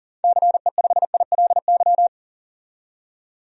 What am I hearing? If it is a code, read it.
Morse code, heard as a keyed tone, about 40 wpm, 700 Hz. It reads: CE5ILY
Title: DXCC-100 Call Signs - 6 Characters - 40wpm